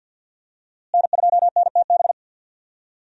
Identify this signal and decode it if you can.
Morse code, heard as a keyed tone, about 47 wpm, 700 Hz. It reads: N2NTB
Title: USA Call Signs - 5 Characters - Repeated 3x - 50wpm